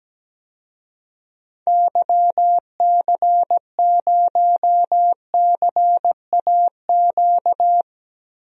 Morse code, heard as a keyed tone, about 17 wpm, 700 Hz. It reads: YC0CAQ
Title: DXCC-100 Call Signs - 6 Characters - Repeated 2x - 17wpm